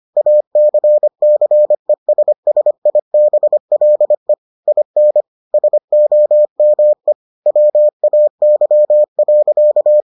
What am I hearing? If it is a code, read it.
Morse code, heard as a keyed tone, about 25 wpm, 600 Hz. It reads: ACCESSIBLE IN SOME WAY.